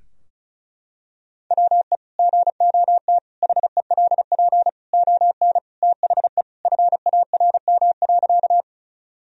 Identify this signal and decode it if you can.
Morse code, heard as a keyed tone, about 36 wpm, 700 Hz. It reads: WE GOT HELP ON THE FARM.
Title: Sentences from Top 1000 Words - Review Part 2 - 35wpm